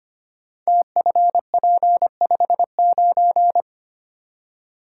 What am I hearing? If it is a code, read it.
Morse code, heard as a keyed tone, about 25 wpm, 700 Hz. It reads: TFP59
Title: Code Groups - 5 Letter:Numbers - 25wpm